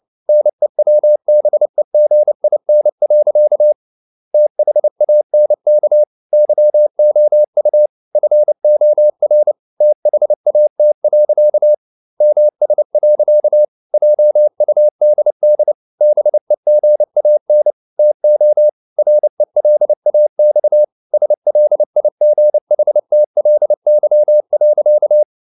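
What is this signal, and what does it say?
Morse code, heard as a keyed tone, about 29 wpm, 600 Hz. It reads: NEWBEGIN. THANK YOU FOR THAT. MS. JUDD BEGAN TO RELAX SLIGHTLY.
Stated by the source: Morse code practice at 28 wpm, text of Ken Follett's novel Never in English, from CW med Gnister.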